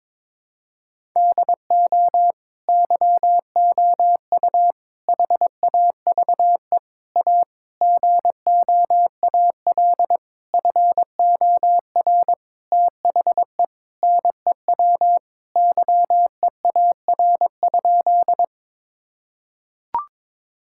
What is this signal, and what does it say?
Morse code, heard as a keyed tone, about 22 wpm, 700 Hz. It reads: DO YOU HAVE A GOAL FOR THE NEW YEAR?
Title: Sentences from Top 1000 Words - Review Part 5 - 22wpm